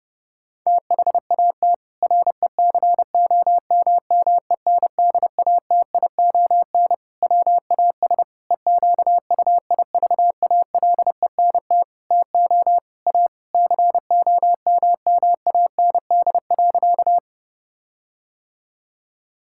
Morse code, heard as a keyed tone, about 30 wpm, 700 Hz. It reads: THAT RECOMMENDATION WAS EQUIVALENT TO A COMMAND.
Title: Sentences from Top 1600 Words 30wpm